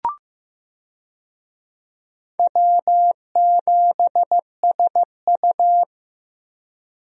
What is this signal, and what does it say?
Morse code, heard as a keyed tone, about 15 wpm, 700 Hz. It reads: W7SU
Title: USA Call Signs - 4 Characters - Repeated 3x - 15wpm